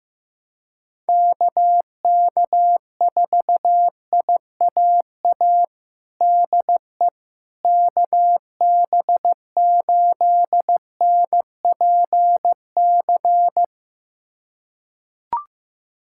Morse code, heard as a keyed tone, about 15 wpm, 700 Hz. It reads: KK4IAA DE KB8NPC
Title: USA Callsign DE Callsign - 6 Characters - 15wpm